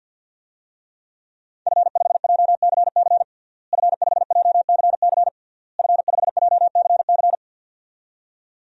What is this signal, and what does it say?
Morse code, heard as a keyed tone, about 46 wpm, 700 Hz. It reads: F5JCC F5JCC F5JCC
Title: DXCC-100 Call Signs - 5 Characters - Repeated 3x - 50wpm